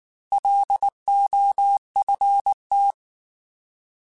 Morse code, heard as a keyed tone, about 19 wpm, 800 Hz. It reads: LOFT
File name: Alphabetical / Alphabetical mp3s / Ls @18